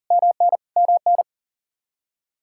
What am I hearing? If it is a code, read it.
Morse code, heard as a keyed tone, about 37 wpm, 700 Hz. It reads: MN MN
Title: Practice QSO - Exchange 1 - Beginning - 40wpm